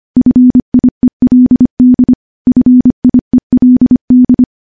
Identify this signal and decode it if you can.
Morse code, heard as a keyed tone, about 25 wpm, 250 Hz. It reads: FIELD FIELD
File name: morse / morse-training-25WPM